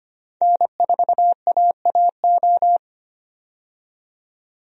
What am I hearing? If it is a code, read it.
Morse code, heard as a keyed tone, about 25 wpm, 690 Hz. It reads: N4AAO
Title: USA Call Signs - 5 Characters - 25wpm